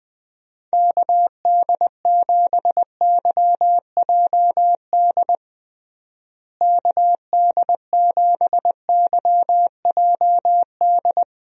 Morse code, heard as a keyed tone, about 20 wpm, 700 Hz. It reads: KD7YJD KD7YJD
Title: USA Call Signs - 6 Characters - Repeated 2x - 20wpm